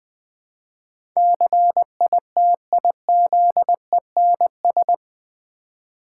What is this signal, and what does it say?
Morse code, heard as a keyed tone, about 20 wpm, 700 Hz. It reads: CITIZENS
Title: Build A Word - 4 Letters At A Time - 11-Letter Words - 20wpm